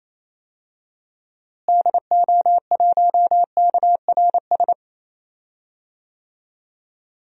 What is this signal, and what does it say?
Morse code, heard as a keyed tone, about 28 wpm, 700 Hz. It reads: DO1KRS
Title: DXCC-100 Call Signs - 6 Characters - Repeated 3x - 28wpm